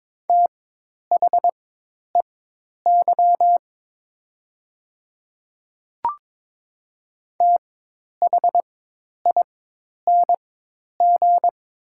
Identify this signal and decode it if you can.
Morse code, heard as a keyed tone, about 22 wpm, spaced as for 10 wpm, 700 Hz. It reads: THEY THING
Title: 4x Character Spacing - Top 100 Words - 22wpm